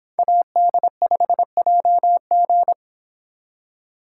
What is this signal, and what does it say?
Morse code, heard as a keyed tone, about 26 wpm, 700 Hz. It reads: AD5JG